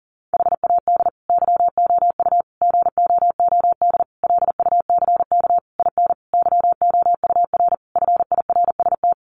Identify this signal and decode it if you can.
Morse code, heard as a keyed tone, about 40 wpm, 700 Hz. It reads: HAD YOU GOOD LUCK IN YOUR FIRST
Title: Sentences from Top 1500 Words 40wpm